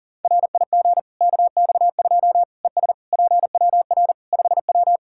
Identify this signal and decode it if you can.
Morse code, heard as a keyed tone, about 40 wpm, 700 Hz. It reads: RIG KX2 ES PWR 5W
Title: K1USN SST QSO Practice 40wpm